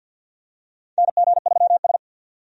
Morse code, heard as a keyed tone, about 50 wpm, 700 Hz. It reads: NG3S
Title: USA Call Signs - 4 Characters - 50wpm